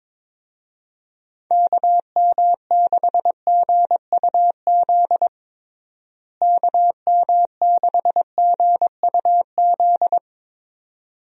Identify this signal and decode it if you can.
Morse code, heard as a keyed tone, about 22 wpm, 700 Hz. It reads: KM6GUZ KM6GUZ
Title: USA Call Signs - 6 Characters - Repeated 2x - 22wpm